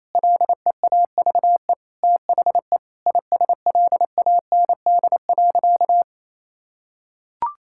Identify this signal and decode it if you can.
Morse code, heard as a keyed tone, about 28 wpm, 700 Hz. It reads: LEAVE THE ISLAND.
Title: Sentences from Top 1000 Words - Review Part 6 - 28wpm